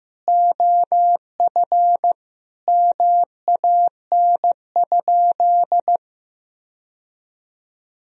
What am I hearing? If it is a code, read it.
Morse code, heard as a keyed tone, about 15 wpm, 700 Hz. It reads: OF MAN?
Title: Sentences from Top 1000 Words - Review Part 2 - 15wpm